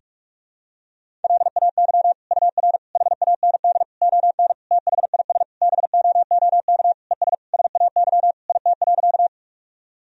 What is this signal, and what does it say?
Morse code, heard as a keyed tone, about 46 wpm, 700 Hz. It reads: LAY UR HAND ON THIS BOOK ES SAY IT.
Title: Sentences from Top 400 Words and Abbreviations 45wpm